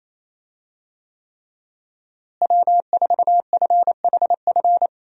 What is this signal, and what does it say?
Morse code, heard as a keyed tone, about 28 wpm, 700 Hz. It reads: W4FHF
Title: USA Call Signs - 5 Characters - Repeated 3x - 28wpm